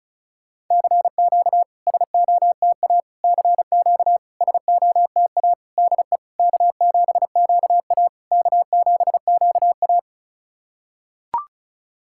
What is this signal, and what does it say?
Morse code, heard as a keyed tone, about 36 wpm, 700 Hz. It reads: CQ SOTA CQ SOTA DE K7QA K7QA
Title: SOTA QSO Practice 35wpm